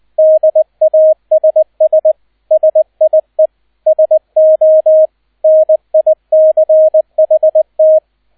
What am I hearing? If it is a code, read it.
Morse code, heard as a keyed tone, about 20 wpm, 630 Hz, with radio noise. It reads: DASS SIE SO NICHT
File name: Erster Streich 20wpm